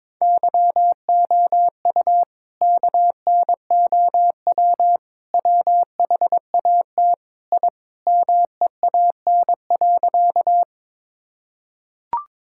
Morse code, heard as a keyed tone, about 22 wpm, 700 Hz. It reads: YOU KNOW WHAT I MEAN.
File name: practice / 22wpm / Sentences with 4-Word Phrases - 22wpm